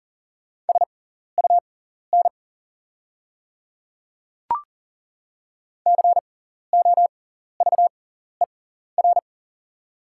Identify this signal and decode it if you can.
Morse code, heard as a keyed tone, about 40 wpm, spaced as for 14 wpm, 690 Hz. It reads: SUN COVER
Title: 6x Character Spacing - Top 300 Words - 40wpm